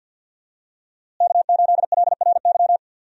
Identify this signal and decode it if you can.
Morse code, heard as a keyed tone, about 46 wpm, 700 Hz. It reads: K8LRY
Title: USA Call Signs - 5 Characters - Repeated 3x - 50wpm